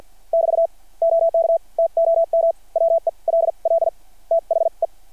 Morse code, heard as a keyed tone, about 53 wpm, spaced as for 41 wpm, 660 Hz. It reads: <BT> OK TOM WELL THE
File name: archived / CRSnet